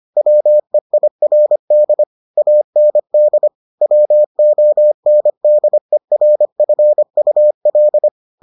Morse code, heard as a keyed tone, about 25 wpm, 600 Hz. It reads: WEIRD AND WONDERFUL